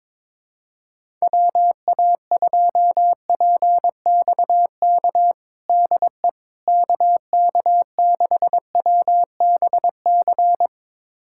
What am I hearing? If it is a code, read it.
Morse code, heard as a keyed tone, about 22 wpm, 700 Hz. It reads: WA2PXK DE KK6WBC
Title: USA Callsign DE Callsign - 6 Characters - 22wpm